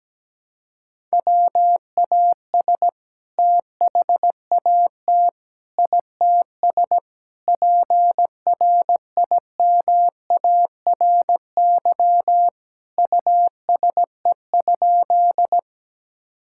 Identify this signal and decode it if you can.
Morse code, heard as a keyed tone, about 17 wpm, 700 Hz. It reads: WAS THAT ITS PRIMARY USE?
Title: Sentences from Top 1000 Words - Review Part 5 - 17wpm